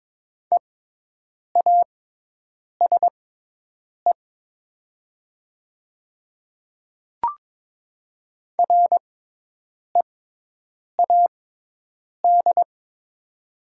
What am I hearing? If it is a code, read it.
Morse code, heard as a keyed tone, about 22 wpm, spaced as for 8 wpm, 700 Hz. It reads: EASE READ
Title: 6x Character Spacing - Top 500 Words - 22wpm